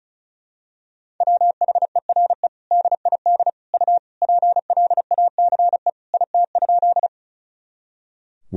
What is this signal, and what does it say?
Morse code, heard as a keyed tone, about 36 wpm, 700 Hz. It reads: WHERE DID U PLACE IT?
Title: Sentences from Top 200 Words and Abbreviations 35wpm